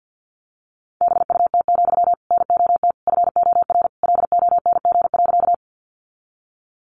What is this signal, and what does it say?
Morse code, heard as a keyed tone, about 50 wpm, 700 Hz. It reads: BUT, NOT FOR LONG.
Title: Sentences from Top 600 Words 50wpm